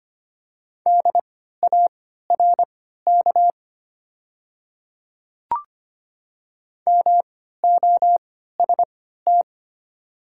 Morse code, heard as a keyed tone, about 25 wpm, spaced as for 14 wpm, 700 Hz. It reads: DARK MOST
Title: 3x Character Spacing - Top 500 Words - 25wpm